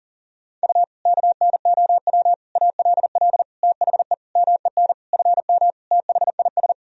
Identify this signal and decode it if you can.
Morse code, heard as a keyed tone, about 40 wpm, 700 Hz. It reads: U KNOW ALL THE MEN FM THIS
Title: Sentences from Top 300 Words and Abbreviations 40wpm